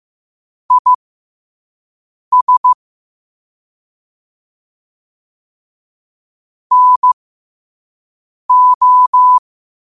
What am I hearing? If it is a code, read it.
Morse code, heard as a keyed tone, about 15 wpm, spaced as for 5 wpm, 1000 Hz. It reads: IS NO
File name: qotdfiles / 5wpm